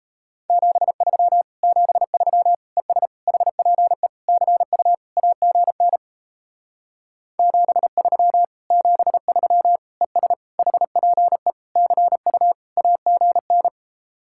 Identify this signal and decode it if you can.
Morse code, heard as a keyed tone, about 37 wpm, 700 Hz. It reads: 73 73 ES HPE CU AGN 73 73 ES HPE CU AGN